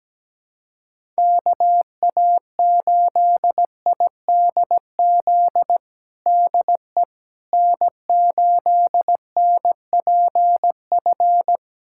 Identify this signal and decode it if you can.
Morse code, heard as a keyed tone, about 17 wpm, 700 Hz. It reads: KA8IDZ DE N8NPF
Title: USA Callsign DE Callsign 17wpm